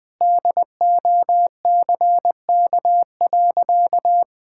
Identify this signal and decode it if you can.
Morse code, heard as a keyed tone, about 21 wpm, 700 Hz. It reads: DOCK.
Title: Sentences with 2-Word Phrases - 20wpm